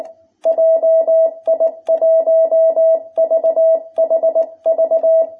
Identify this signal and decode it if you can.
Morse code, heard as a keyed tone, about 19 wpm, 660 Hz. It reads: JI1VHV
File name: jh8cbh / 2021cw